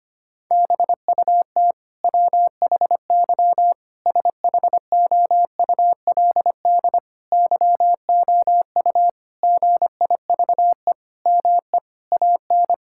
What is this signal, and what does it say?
Morse code, heard as a keyed tone, about 25 wpm, 700 Hz. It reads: BUT WHY SHOULD YOU GIVE ME AN
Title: Sentences from Top 1000 Words - Review Part 6 - 25wpm